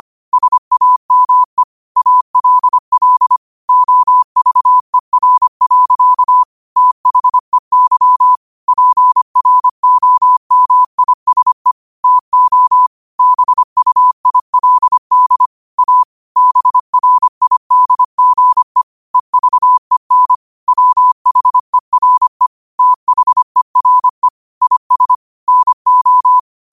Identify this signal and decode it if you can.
Morse code, heard as a keyed tone, about 25 wpm, 1000 Hz. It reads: SAME ALL OVER. THEY PROMISE TO BUILD A BRIDGE EVEN WHERE THERE IS NO
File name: qotdfiles / 25wpm